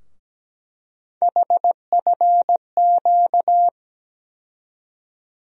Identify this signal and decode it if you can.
Morse code, heard as a keyed tone, about 17 wpm, 700 Hz. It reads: HFQ
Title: USA Call Sign Postfixes - 3 Letters 17wpm